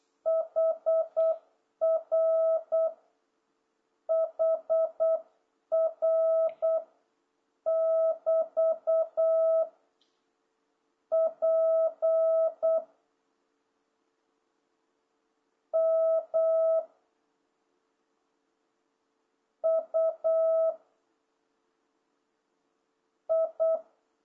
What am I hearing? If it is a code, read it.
Morse code, heard as a keyed tone, about 8 wpm, 660 Hz. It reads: HR HR <BT> P M U I